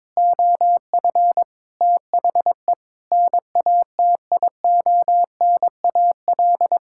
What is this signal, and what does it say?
Morse code, heard as a keyed tone, about 22 wpm, 700 Hz. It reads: OF THE NATIONAL
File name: practice / 22wpm / Sentences with 3-Word Phrases - 22wpm